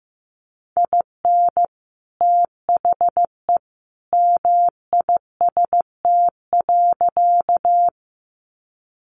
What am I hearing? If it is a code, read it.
Morse code, heard as a keyed tone, about 15 wpm, 700 Hz. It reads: IN THE MIST.
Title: A Sound Of Thunder 15wpm